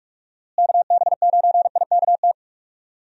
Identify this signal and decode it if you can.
Morse code, heard as a keyed tone, about 46 wpm, 700 Hz. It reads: KB9IKT
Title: USA Call Signs - 6 Characters - Repeated 3x - 45wpm